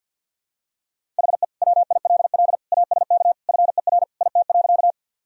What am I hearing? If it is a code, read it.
Morse code, heard as a keyed tone, about 50 wpm, 700 Hz. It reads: HE WILL ASK FER IT.
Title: Sentences from Top 200 Words and Abbreviations 50wpm